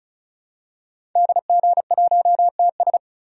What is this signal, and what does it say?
Morse code, heard as a keyed tone, about 36 wpm, 700 Hz. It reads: DG1TS
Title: DXCC-100 Call Signs - 5 Characters - Repeated 2x - 35wpm